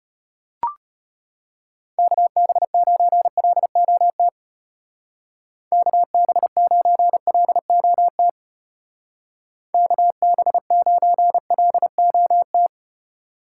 Morse code, heard as a keyed tone, about 32 wpm, 700 Hz. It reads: KB9LOT KB9LOT KB9LOT